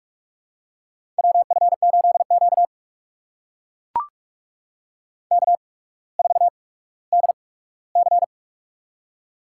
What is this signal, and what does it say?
Morse code, heard as a keyed tone, about 44 wpm, spaced as for 19 wpm, 700 Hz. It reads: WF8Q K4DC
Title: USA Call Signs - 4 Characters - 8x Character Spacing - 45wpm